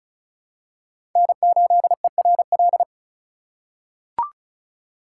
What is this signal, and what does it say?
Morse code, heard as a keyed tone, about 36 wpm, 700 Hz. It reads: N8ERL
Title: USA Call Signs - 5 Characters - 3x Character Spacing - 35wpm